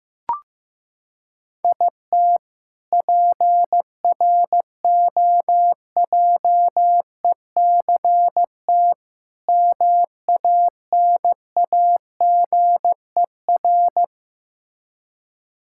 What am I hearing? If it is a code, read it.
Morse code, heard as a keyed tone, about 15 wpm, 700 Hz. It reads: IT PROJECT MANAGER
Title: K1USN SST QSO Practice 15wpm